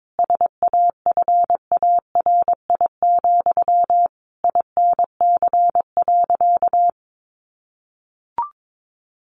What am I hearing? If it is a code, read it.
Morse code, heard as a keyed tone, about 22 wpm, 700 Hz. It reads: SAFARI, INC.
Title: A Sound Of Thunder 22wpm